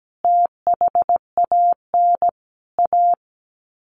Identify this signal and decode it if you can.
Morse code, heard as a keyed tone, about 17 wpm, 700 Hz. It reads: THAN A
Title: Sets of 4 Words 17wpm